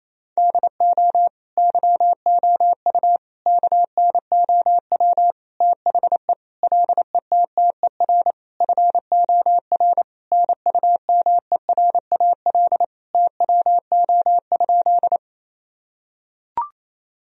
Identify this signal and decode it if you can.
Morse code, heard as a keyed tone, about 27 wpm, 700 Hz. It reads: DO YOU KNOW THE LETTER FOR NUMERAL TWO?
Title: Sentences from Top 1000 Words - Review Part 2 - 28wpm